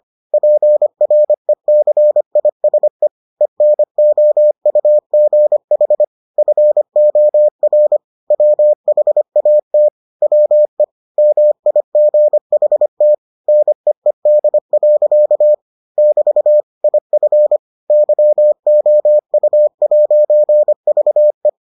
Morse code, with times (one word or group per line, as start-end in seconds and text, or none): 0.3-3.1 PRECISE
3.4-6.0 ENOUGH
6.4-8.0 FOR
8.3-9.9 WHAT
10.2-10.8 WE
11.2-13.1 MIGHT
13.5-15.5 NEED.
16.0-16.6 <BT>
16.8-17.6 IF
17.9-21.5 YOU'VE